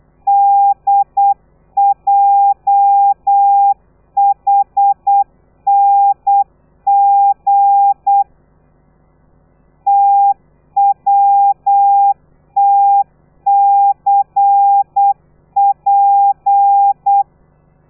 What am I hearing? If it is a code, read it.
Morse code, heard as a keyed tone, about 8 wpm, 800 Hz. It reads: DJHNG TWTCP
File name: telegrafia / i2tfj / Lez13 35